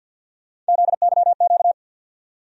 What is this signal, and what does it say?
Morse code, heard as a keyed tone, about 51 wpm, 700 Hz. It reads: ZYQ
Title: USA Call Sign Postfixes - 3 Letters 50wpm